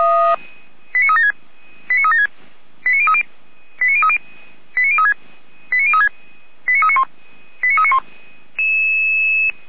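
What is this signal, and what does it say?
Five-tone selective call (ZVEI), heard as quick runs of five tones, tones of about 70 ms. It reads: 89376 89376 89030 89030 89036 89036 89391 89391
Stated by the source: ZVEI five-tone call 89042 received by SDR.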